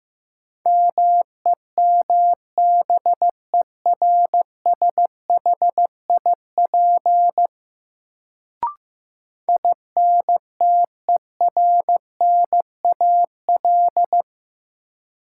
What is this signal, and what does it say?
Morse code, heard as a keyed tone, about 15 wpm, 700 Hz. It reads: MEMBERSHIP INTERNAL
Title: Sentences from Top 1400 Words - New Word Drill Alt - 15wpm